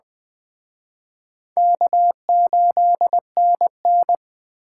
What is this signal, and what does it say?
Morse code, heard as a keyed tone, about 21 wpm, 700 Hz. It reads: K8NN
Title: Speed Racing - USA 1x2 Call Signs 17wpm